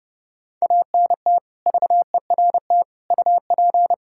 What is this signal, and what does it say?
Morse code, heard as a keyed tone, about 30 wpm, 700 Hz. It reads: ANT VERT UP
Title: K1USN SST QSO Practice 30wpm